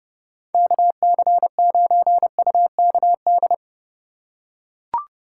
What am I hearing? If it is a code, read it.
Morse code, heard as a keyed tone, about 32 wpm, 700 Hz. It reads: KC9UKD
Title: USA Call Signs - 6 Characters - 3x Character Spacing - 30wpm